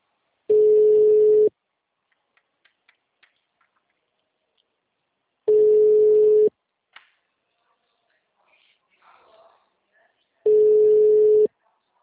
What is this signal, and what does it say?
Ringback tone (Europe, 425 Hz), with speech.